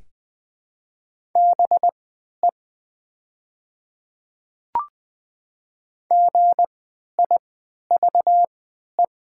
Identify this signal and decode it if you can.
Morse code, heard as a keyed tone, about 20 wpm, spaced as for 11 wpm, 700 Hz. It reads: BE GIVE
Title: 3x Character Spacing - Top 300 Words - 20wpm